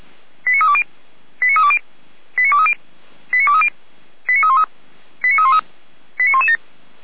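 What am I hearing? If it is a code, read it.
Five-tone selective call (ZVEI), heard as quick runs of five tones, tones of about 70 ms. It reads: 89320 89320 89230 89230 89313 89313 89117
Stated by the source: Received on SDR, ZVEI five-tone call, address 89250.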